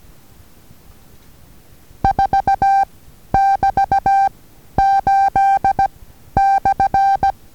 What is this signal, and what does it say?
Morse code, heard as a keyed tone, about 16 wpm, 790 Hz. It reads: 4 <BT> 8 /
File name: Taviro HA6VA A